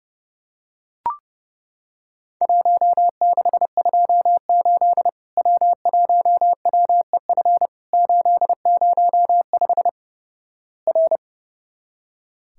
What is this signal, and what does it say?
Morse code, heard as a keyed tone, about 32 wpm, 700 Hz. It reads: 1628 W1WEF 805 R
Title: Worked All Europe DX Contest - QTC Exchange - 30wpm